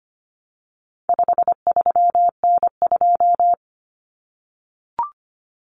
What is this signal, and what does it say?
Morse code, heard as a keyed tone, about 25 wpm, 700 Hz. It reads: 53N2
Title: Common Cut Numbers - 4-Digit 25wpm